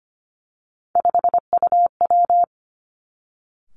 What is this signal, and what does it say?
Morse code, heard as a keyed tone, about 25 wpm, 700 Hz. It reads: 5UW
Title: Three Challenging Letter-Number Combinations 25wpm